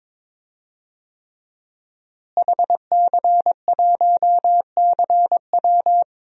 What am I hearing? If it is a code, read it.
Morse code, heard as a keyed tone, about 22 wpm, 700 Hz. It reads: HC1CW
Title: DXCC-100 Call Signs - 5 Characters - Repeated 3x - 22wpm